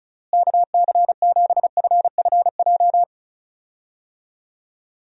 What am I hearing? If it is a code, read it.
Morse code, heard as a keyed tone, about 36 wpm, 700 Hz. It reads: KC7FFJ
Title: USA Call Signs - 6 Characters - Repeated 2x - 35wpm